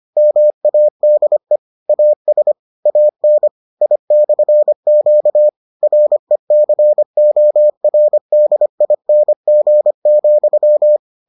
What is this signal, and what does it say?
Morse code, heard as a keyed tone, about 25 wpm, 600 Hz. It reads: MADE AS AN I/Q RECORDING,